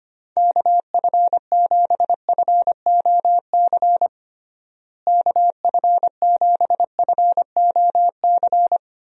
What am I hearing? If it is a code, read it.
Morse code, heard as a keyed tone, about 25 wpm, 700 Hz. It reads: KF7FOC KF7FOC
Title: USA Call Signs - 6 Characters - Repeated 2x - 25wpm